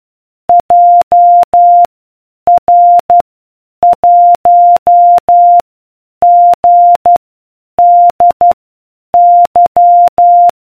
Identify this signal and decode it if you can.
Morse code, heard as a keyed tone, about 12 wpm, spaced as for 8 wpm, 700 Hz. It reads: JR1GDY